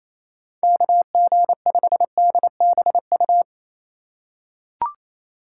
Morse code, heard as a keyed tone, about 27 wpm, 700 Hz. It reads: KG5DBU
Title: USA Call Signs - 4x Character Spacing - 28wpm